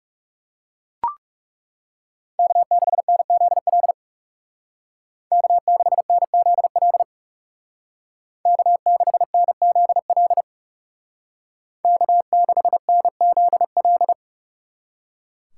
Morse code, heard as a keyed tone, about 40 wpm, 700 Hz. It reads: K6NZL K6NZL K6NZL K6NZL